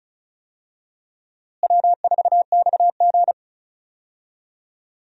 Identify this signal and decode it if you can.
Morse code, heard as a keyed tone, about 36 wpm, 700 Hz. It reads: W4XG
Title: USA Call Signs - 4 Characters - Repeated 2x - 35wpm